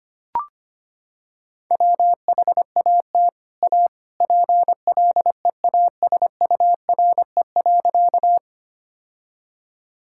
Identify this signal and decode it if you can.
Morse code, heard as a keyed tone, about 25 wpm, 700 Hz. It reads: WHAT A PLEASURE.
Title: Sentences from Top 1600 Words 25wpm